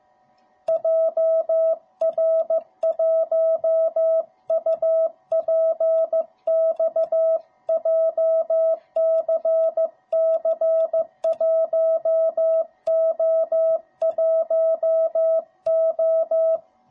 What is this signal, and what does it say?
Morse code, heard as a keyed tone, about 15 wpm, 660 Hz. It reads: JR1UPX JCC1O1O